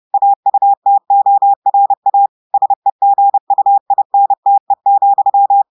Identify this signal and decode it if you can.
Morse code, heard as a keyed tone, about 30 wpm, 800 Hz. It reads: AUTORA SEGUINTE,
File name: audios / portugais / portugais texte en clair 1 vitesse 30 mots minutes